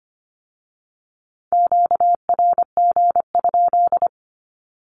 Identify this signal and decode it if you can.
Morse code, heard as a keyed tone, about 25 wpm, 700 Hz. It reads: QRG?
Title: Q-Codes with Spoken Meaning - 25wpm